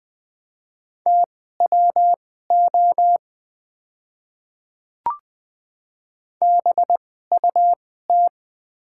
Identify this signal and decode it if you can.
Morse code, heard as a keyed tone, about 20 wpm, spaced as for 14 wpm, 700 Hz. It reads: TWO BUT
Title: 2x Character Spacing - Top 100 Words - 20wpm